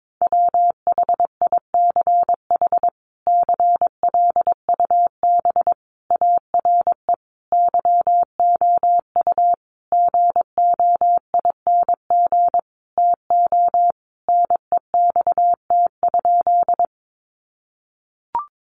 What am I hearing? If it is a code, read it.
Morse code, heard as a keyed tone, about 22 wpm, 700 Hz. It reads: WHICH CLUB ARE YOU GOING TO NEXT?
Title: Sentences from Top 1200 Words 22wpm